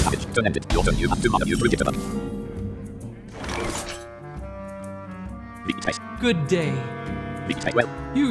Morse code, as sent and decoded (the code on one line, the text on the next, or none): none